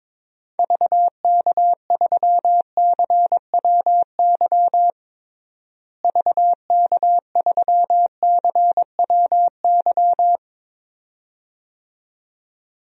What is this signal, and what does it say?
Morse code, heard as a keyed tone, about 22 wpm, 700 Hz. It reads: VK3CWY VK3CWY
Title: DXCC-100 Call Signs - 6 Characters - Repeated 2x - 22wpm